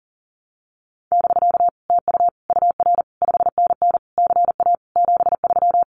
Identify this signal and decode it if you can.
Morse code, heard as a keyed tone, about 40 wpm, 700 Hz. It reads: <BK> TU UR 5NN CA 73
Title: Common QSO Phrases Encore 40wpm